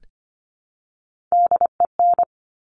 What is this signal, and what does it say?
Morse code, heard as a keyed tone, about 25 wpm, 700 Hz. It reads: DEN
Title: Top Three Letter Combinations in Words 25wpm